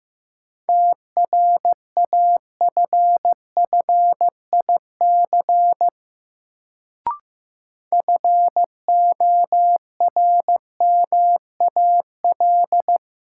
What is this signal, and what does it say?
Morse code, heard as a keyed tone, about 15 wpm, 700 Hz. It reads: TRAFFIC FORMAL
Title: Sentences from Top 1300 Words - New Word Drill Alt - 15wpm